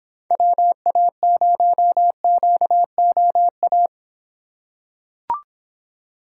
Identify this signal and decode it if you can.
Morse code, heard as a keyed tone, about 26 wpm, 700 Hz. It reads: WA0QOA